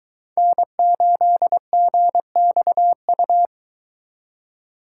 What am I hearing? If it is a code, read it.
Morse code, heard as a keyed tone, about 22 wpm, 700 Hz. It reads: N8GXU